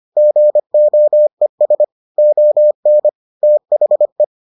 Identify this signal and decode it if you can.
Morse code, heard as a keyed tone, about 25 wpm, 600 Hz. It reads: GOES ON THE